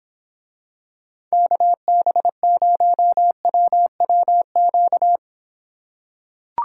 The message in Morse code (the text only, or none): KB0WWQ